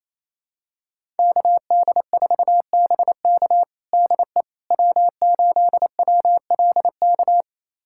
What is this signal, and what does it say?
Morse code, heard as a keyed tone, about 28 wpm, 700 Hz. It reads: KD4BK DE W8WLK
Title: USA Callsign DE Callsign 28wpm